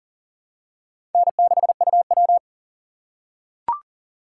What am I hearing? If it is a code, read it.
Morse code, heard as a keyed tone, about 37 wpm, 700 Hz. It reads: N6UW
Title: USA Call Signs - 6x Character Spacing - 40wpm